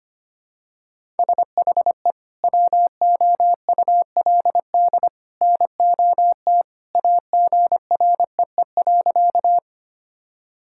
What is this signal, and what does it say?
Morse code, heard as a keyed tone, about 25 wpm, 700 Hz. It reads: SHE WOULD NOT AGREE.
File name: practice / 25wpm / Sentences with 2-Word Phrases - 25wpm